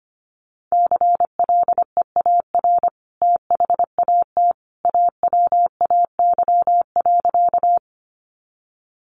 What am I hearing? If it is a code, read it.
Morse code, heard as a keyed tone, about 25 wpm, 700 Hz. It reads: CLEAR THAT AWAY.
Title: Sentences from Top 600 Words 25wpm